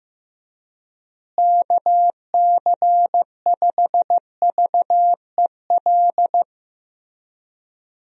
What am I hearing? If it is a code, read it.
Morse code, heard as a keyed tone, about 15 wpm, 700 Hz. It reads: KC5VEL
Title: USA Call Signs - 6 Characters - Repeated 3x - 15wpm